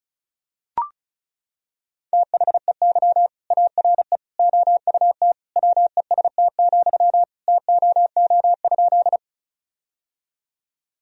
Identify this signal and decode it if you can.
Morse code, heard as a keyed tone, about 36 wpm, 700 Hz. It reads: THEY ARE OUT WEST, TOO?
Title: Sentences from Top 1000 Words - Review Part 2 - 35wpm